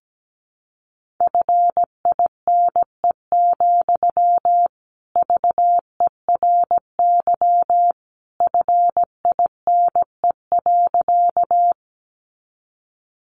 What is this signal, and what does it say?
Morse code, heard as a keyed tone, about 17 wpm, 700 Hz. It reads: FINE, VERY FINE.
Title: Sentences from Top 500 Words 17wpm